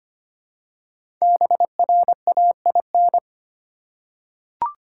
Morse code, heard as a keyed tone, about 25 wpm, 700 Hz. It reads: BRAIN
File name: 5 Letter Words - First 3 Letters Given - 25wpm